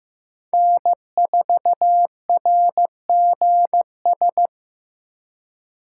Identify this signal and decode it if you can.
Morse code, heard as a keyed tone, about 15 wpm, 700 Hz. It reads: N4RGS
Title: USA Call Signs - 5 Characters - 3x Character Spacing - 15wpm